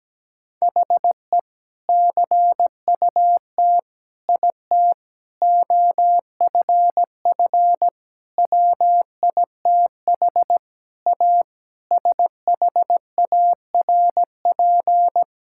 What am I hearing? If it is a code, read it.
Morse code, heard as a keyed tone, about 17 wpm, 700 Hz. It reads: HE CUT IT OFF WITH A SHARP
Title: Sentences from Top 1600 Words 17wpm